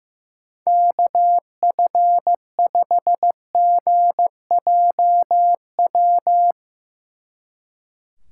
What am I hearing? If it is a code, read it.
Morse code, heard as a keyed tone, about 15 wpm, 700 Hz. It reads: KF5GJW
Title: Speed Racing - USA 2x3 Call Signs 15wpm